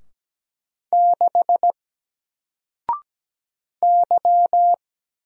Morse code, heard as a keyed tone, about 17 wpm, 700 Hz. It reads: 6 Y
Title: Single Character (Letter:Number:Punctuation:Prosign) 17wpm